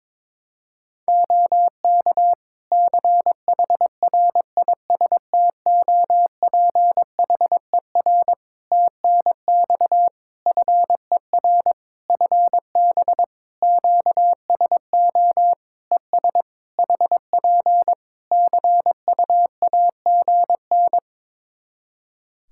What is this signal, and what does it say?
Morse code, heard as a keyed tone, about 22 wpm, 700 Hz. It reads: OK CHRISTOPHER TNX FER FB QSO ES HP CUAGN
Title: Practice QSO - Exchange 4 - Ending - 22wpm